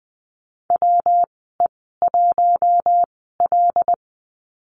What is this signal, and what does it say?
Morse code, heard as a keyed tone, about 20 wpm, spaced as for 15 wpm, 700 Hz. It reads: WE1L
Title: 122 Lesson 20g - Number 3 - Call Signs 20wpm